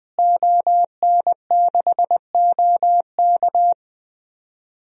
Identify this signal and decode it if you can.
Morse code, heard as a keyed tone, about 20 wpm, 700 Hz. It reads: ON6OK
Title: DXCC-100 Call Signs - 5 Characters - Repeated 2x - 20wpm